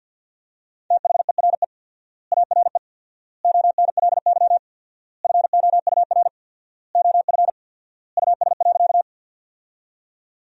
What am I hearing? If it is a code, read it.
Morse code, heard as a keyed tone, about 48 wpm, 700 Hz. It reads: THERE ARE ONLY FOUR OF US.
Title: 4x Word-Spacing - Sentences from Top 300 Words - 50wpm